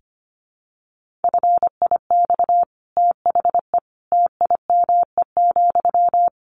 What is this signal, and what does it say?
Morse code, heard as a keyed tone, about 25 wpm, 700 Hz. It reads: FIX THE TIME,
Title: Sentences from Top 1400 Words 25wpm